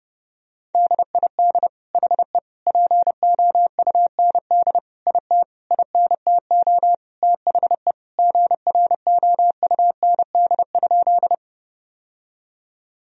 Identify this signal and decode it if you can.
Morse code, heard as a keyed tone, about 30 wpm, 700 Hz. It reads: DID HE POUND IT INTO THE GROUND?
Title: Sentences from Top 1000 Words - Review Part 2 - 30wpm